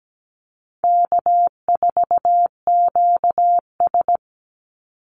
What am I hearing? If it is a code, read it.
Morse code, heard as a keyed tone, about 17 wpm, 700 Hz. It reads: K4QS
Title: NA SOTA Activator Callsigns 17wpm